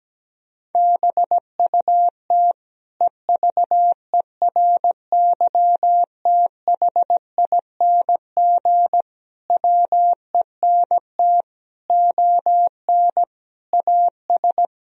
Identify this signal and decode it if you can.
Morse code, heard as a keyed tone, about 17 wpm, 700 Hz. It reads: BUT EVERYTHING WENT ON AS
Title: Sentences from Top 1000 Words - Review Part 2 - 17wpm